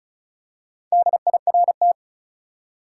Morse code, heard as a keyed tone, about 34 wpm, 700 Hz. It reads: DIRT
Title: Four Letter Words - 35wpm